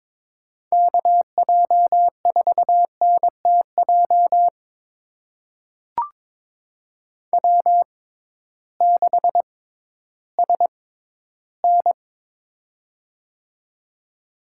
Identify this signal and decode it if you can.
Morse code, heard as a keyed tone, about 21 wpm, spaced as for 13 wpm, 700 Hz. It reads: KJ4NTJ W6SN
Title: USA Call Signs - 6x Character Spacing - 22wpm